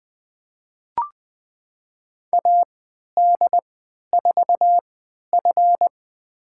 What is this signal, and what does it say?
Morse code, heard as a keyed tone, about 20 wpm, spaced as for 11 wpm, 700 Hz. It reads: AD4F
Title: USA Call Signs - 4 Characters - 3x Character Spacing - 20wpm